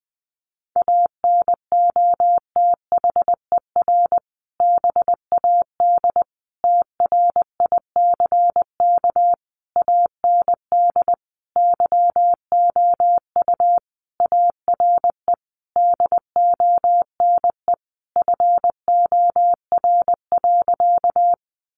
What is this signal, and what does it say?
Morse code, heard as a keyed tone, about 20 wpm, 700 Hz. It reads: ANOTHER BAD TRICK AND YOU ARE DONE FOR.
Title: Sentences from Top 1500 Words 20wpm